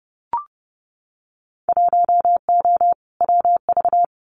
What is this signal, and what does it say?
Morse code, heard as a keyed tone, about 31 wpm, 700 Hz. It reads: 1O WV
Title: Winter Field Day Exchanges 30wpm